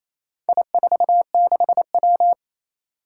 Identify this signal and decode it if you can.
Morse code, heard as a keyed tone, about 28 wpm, 700 Hz. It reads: I46W
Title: Four Challenging Letter-Number Combinations 28wpm